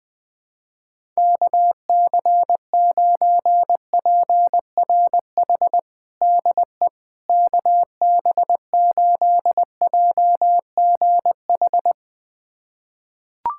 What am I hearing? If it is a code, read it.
Morse code, heard as a keyed tone, about 21 wpm, 700 Hz. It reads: KC9PRH DE KB8JGH
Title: USA Callsign DE Callsign - 6 Characters - 20wpm